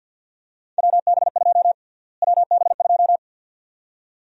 Morse code, heard as a keyed tone, about 51 wpm, 700 Hz. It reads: WB2 WB2
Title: USA Call Sign Prefixes - Repeated 2x - 50wpm